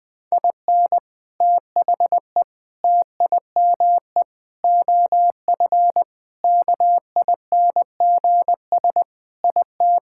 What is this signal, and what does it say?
Morse code, heard as a keyed tone, about 21 wpm, 700 Hz. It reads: IN THE TIME OF KINGS IT
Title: Sentences with 3-Word Phrases - 20wpm